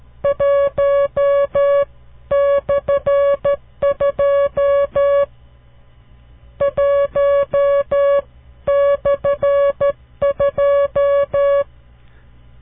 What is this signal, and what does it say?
Morse code, heard as a keyed tone, about 12 wpm, 560 Hz. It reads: 1/2 1/2
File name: CW Learning Programs / farnsworth / unit 5